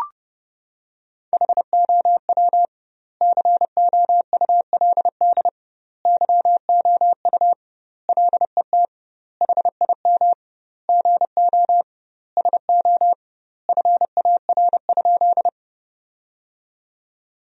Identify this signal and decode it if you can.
Morse code, heard as a keyed tone, about 30 wpm, 700 Hz. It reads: HOW COULD YOU LET HIM GO SO FAR?
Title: 2x Word-Spacing - Sentences from Top 300 Words - 30wpm